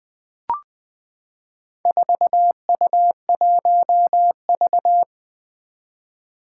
Morse code, heard as a keyed tone, about 20 wpm, 700 Hz. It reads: 4U1V
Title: Four Challenging Letter-Number Combinations 20wpm